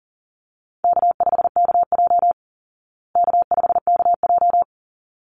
Morse code, heard as a keyed tone, about 40 wpm, 700 Hz. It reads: K5KJ K5KJ
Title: Call Signs Repeated 3x - 40wpm